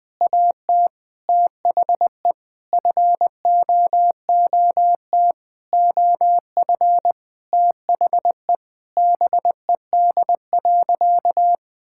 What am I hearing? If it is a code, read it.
Morse code, heard as a keyed tone, about 20 wpm, 700 Hz. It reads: AT THE FOOT OF THE BED.
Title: Sentences with 5-Word Phrases - Spoken Beginning - 20wpm